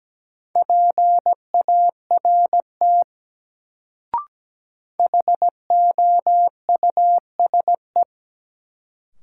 Morse code, heard as a keyed tone, about 17 wpm, 700 Hz. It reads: PART HOUSE
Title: Sets of 2 Words - New Word Drill - 17wpm